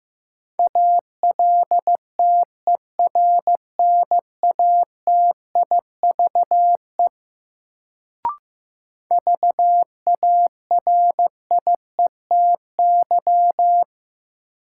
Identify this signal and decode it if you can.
Morse code, heard as a keyed tone, about 15 wpm, 700 Hz. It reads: ALTERNATIVE VARIETY
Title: Sentences from Top 1000 Words - Review Part 4 - New Word Drill - 15wpm